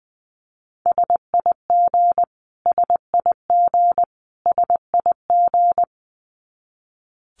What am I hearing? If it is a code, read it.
Morse code, heard as a keyed tone, about 20 wpm, 700 Hz. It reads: SIG SIG SIG
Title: Abbreviations Repeated 3x - 20wpm